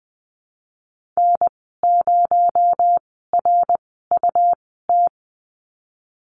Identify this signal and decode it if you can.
Morse code, heard as a keyed tone, about 20 wpm, 700 Hz. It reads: N 0 R U T
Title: 238 Lesson 38g - Letter q - Call Signs 20wpm